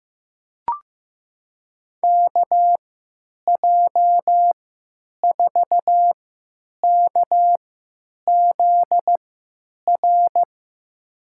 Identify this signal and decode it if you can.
Morse code, heard as a keyed tone, about 15 wpm, spaced as for 9 wpm, 700 Hz. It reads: KJ4KZR
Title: USA Call Signs - 6 Characters - 3x Character Spacing - 15wpm